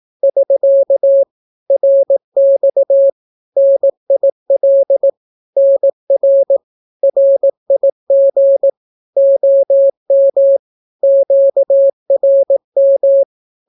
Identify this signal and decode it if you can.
Morse code, heard as a keyed tone, about 18 wpm, 550 Hz. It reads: <SK> RX NIL NR RIG OM QRM
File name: naukapodstaw / skroty / skroty